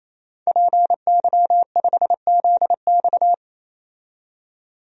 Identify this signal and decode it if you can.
Morse code, heard as a keyed tone, about 27 wpm, 700 Hz. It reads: PY5ZX
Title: DXCC-100 Call Signs - 5 Characters - 28wpm